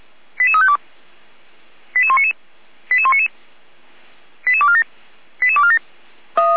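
Five-tone selective call (ZVEI), heard as quick runs of five tones, tones of about 70 ms. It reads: 88352 88199 88199 88247 88247
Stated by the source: ZVEI five-tone call 88103 received on SDR, recorded by feuerwehr-orlamuende.org fire brigade.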